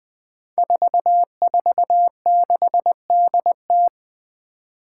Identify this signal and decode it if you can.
Morse code, heard as a keyed tone, about 20 wpm, 700 Hz. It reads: 446DT
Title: Five Challenging Letter-Number Combinations 20wpm